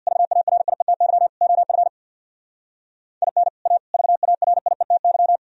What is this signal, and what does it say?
Morse code, heard as a keyed tone, about 60 wpm, 700 Hz. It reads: VARIETY OF IN A VARIETY
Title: Speed-Racing - Sets of 4 Words 50wpm